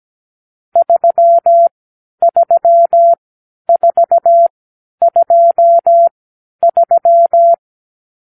Morse code, heard as a keyed tone, about 17 wpm, spaced as for 10 wpm, 680 Hz. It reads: 33423
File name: k7qo / Lesson 46 (1-4)